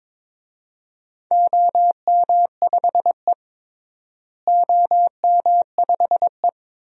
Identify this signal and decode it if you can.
Morse code, heard as a keyed tone, about 22 wpm, 700 Hz. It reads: OM5E OM5E
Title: DXCC-100 Call Signs - 4 Characters - Repeated 2x - 22wpm